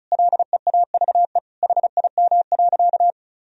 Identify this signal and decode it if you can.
Morse code, heard as a keyed tone, about 34 wpm, 700 Hz. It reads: LEAVE HIM.
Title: Sentences from Top 1000 Words - Review Part 4 - 35wpm